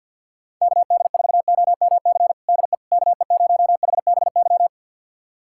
Morse code, heard as a keyed tone, about 46 wpm, 700 Hz. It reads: KD4OMC DE KE0HBY
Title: USA Callsign DE Callsign - 6 Characters - 50wpm